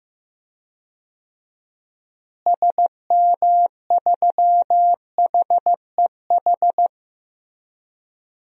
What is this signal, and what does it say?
Morse code, heard as a keyed tone, about 15 wpm, 700 Hz. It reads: SM3HEH
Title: DXCC-100 Call Signs - 6 Characters - Repeated 2x - 15wpm